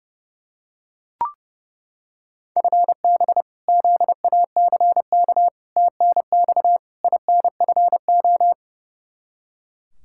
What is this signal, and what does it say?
Morse code, heard as a keyed tone, about 30 wpm, 700 Hz. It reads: FB ZACK TNX INFO
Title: Practice QSO - Exchange 3 - More Info - 30wpm